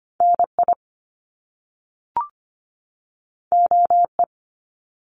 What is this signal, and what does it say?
Morse code, heard as a keyed tone, about 25 wpm, 700 Hz. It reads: NI OE
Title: Top Two Letter Combinations in Words 25wpm